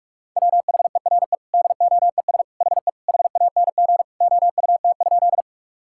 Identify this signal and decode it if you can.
Morse code, heard as a keyed tone, about 46 wpm, 700 Hz. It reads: WHERE DOES HE HANG OUT?
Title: Sentences from Top 1600 Words 45wpm